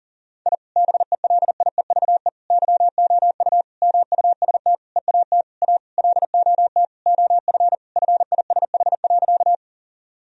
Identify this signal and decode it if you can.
Morse code, heard as a keyed tone, about 37 wpm, 700 Hz. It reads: I BELIEVE YOU MUST EAT A LOT OF FISH.
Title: Sentences from Top 1000 Words - Review Part 3 - 40wpm